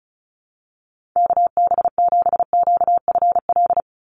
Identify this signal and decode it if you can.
Morse code, heard as a keyed tone, about 34 wpm, 700 Hz. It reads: KB7QFL
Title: USA Call Signs 35wpm